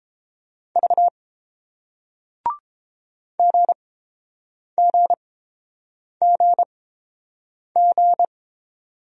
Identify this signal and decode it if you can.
Morse code, heard as a keyed tone, about 30 wpm, spaced as for 7 wpm, 700 Hz. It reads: V GGGG